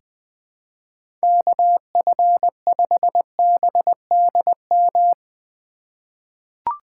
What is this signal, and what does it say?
Morse code, heard as a keyed tone, about 20 wpm, 700 Hz. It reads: KF5BDM
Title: USA Call Signs - 6 Characters - 4x Character Spacing - 20wpm